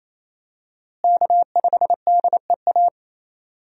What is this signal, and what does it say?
Morse code, heard as a keyed tone, about 28 wpm, 700 Hz. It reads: K5DEA
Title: USA Call Signs - 5 Characters - Repeated 2x - 28wpm